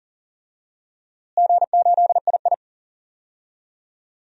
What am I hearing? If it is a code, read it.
Morse code, heard as a keyed tone, about 40 wpm, 700 Hz. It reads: G8II
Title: DXCC-100 Call Signs - 4 Characters - Repeated 3x - 40wpm